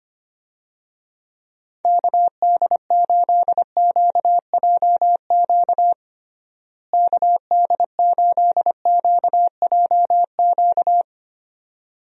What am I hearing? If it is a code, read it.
Morse code, heard as a keyed tone, about 25 wpm, 700 Hz. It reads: KD8QJQ KD8QJQ
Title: USA Call Signs - 6 Characters - Repeated 2x - 25wpm